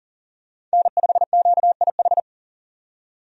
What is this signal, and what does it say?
Morse code, heard as a keyed tone, about 40 wpm, 700 Hz. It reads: N5QIH
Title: USA Call Signs - 5 Characters - 40wpm